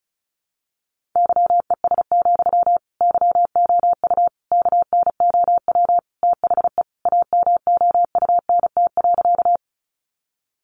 Morse code, heard as a keyed tone, about 34 wpm, 700 Hz. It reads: YES, YOU KNOW THE AMOUNT.
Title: Sentences from Top 700 Words 35wpm